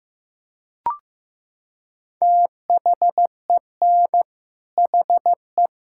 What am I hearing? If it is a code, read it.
Morse code, heard as a keyed tone, about 15 wpm, 700 Hz. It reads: THEN HE
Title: Sentences from Top 1600 Words 15wpm